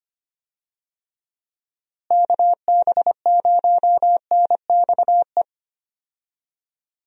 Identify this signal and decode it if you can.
Morse code, heard as a keyed tone, about 25 wpm, 700 Hz. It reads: KB0NXE
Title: USA Call Signs - 6 Characters - Repeated 3x - 25wpm